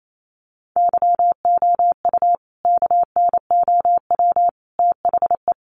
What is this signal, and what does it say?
Morse code, heard as a keyed tone, about 28 wpm, 700 Hz. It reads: YOU KNOW THE
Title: Sentences from Top 1000 Words 28wpm